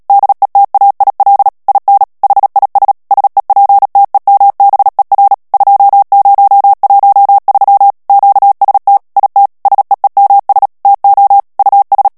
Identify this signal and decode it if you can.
Morse code, heard as a keyed tone, about 37 wpm, 800 Hz. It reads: DETAIL IN HIS SEPTEMBER 2013 QST IT SEEMS TO US
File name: morse mp3 / ANGLAIS texte clair 3 vitesse 35 mots minute